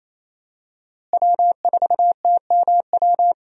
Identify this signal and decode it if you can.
Morse code, heard as a keyed tone, about 28 wpm, 700 Hz. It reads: W4TMW
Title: USA Call Signs - 5 Characters - Repeated 2x - 28wpm